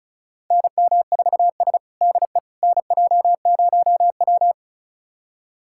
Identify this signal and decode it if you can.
Morse code, heard as a keyed tone, about 36 wpm, 700 Hz. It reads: NM4S DE NJ0W
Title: USA Callsign DE Callsign - 4 Characters - 35wpm